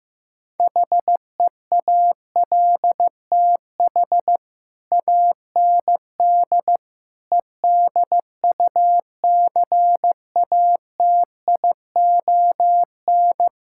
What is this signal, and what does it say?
Morse code, heard as a keyed tone, about 15 wpm, 700 Hz. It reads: HEALTH AND EDUCATION
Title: Sentences from Top 1000 Words - Review Part 6 - 15wpm